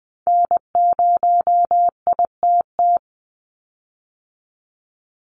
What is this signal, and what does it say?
Morse code, heard as a keyed tone, about 20 wpm, 700 Hz. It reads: N0ITT
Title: USA Call Signs 20wpm